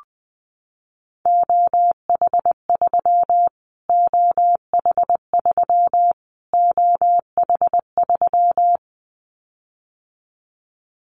Morse code, heard as a keyed tone, about 20 wpm, 700 Hz. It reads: OH3 OH3 OH3
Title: Callsign Prefixes for Top 25 DXCC - Repeated 3x - 20wpm